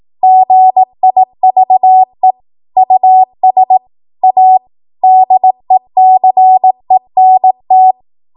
Morse code, heard as a keyed tone, about 18 wpm, 750 Hz. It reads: GIVE US A DECENT
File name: Archive / 18 WPM / 18WPM